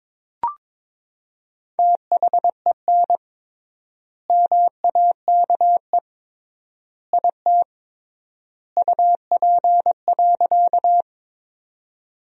Morse code, heard as a keyed tone, about 22 wpm, 700 Hz. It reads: THEN MAKE IT UP.
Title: 3x Word-Spacing - Sentences from Top 100 Words - 22wpm